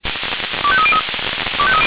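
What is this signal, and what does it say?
Five-tone selective call (ZVEI), heard as quick runs of five tones, tones of about 70 ms. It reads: 25303
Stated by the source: Five-tone ZVEI selective call received on an SDR.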